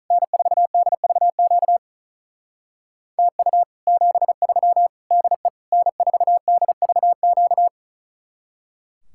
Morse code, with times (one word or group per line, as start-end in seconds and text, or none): 0.1-1.8 N4DVQ
3.2-3.6 TU
3.9-4.9 73
5.1-5.5 DE
5.7-7.7 N4DVQ